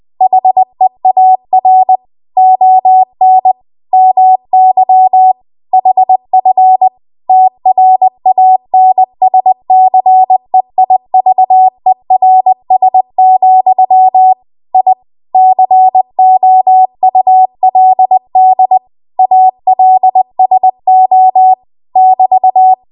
Morse code, heard as a keyed tone, about 20 wpm, 750 Hz. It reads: HEAR ON MY HF TRANSCEIVERS, I COULD ALSO <BT>
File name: Archive / 20 WPM / 20WPM